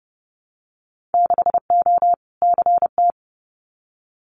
Morse code, heard as a keyed tone, about 30 wpm, 700 Hz. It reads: 6O CT
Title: Winter Field Day Exchanges Encore 30wpm